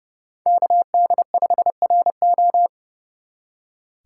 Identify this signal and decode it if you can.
Morse code, heard as a keyed tone, about 35 wpm, 700 Hz. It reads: KD5RO
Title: Speed Racing - USA 2x2 Call Signs 30wpm